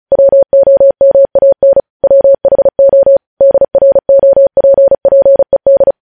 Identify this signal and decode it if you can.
Morse code, heard as a keyed tone, about 36 wpm, 550 Hz. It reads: WOMAN WHO DROPPED
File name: podcast / News headlines-1x-35x35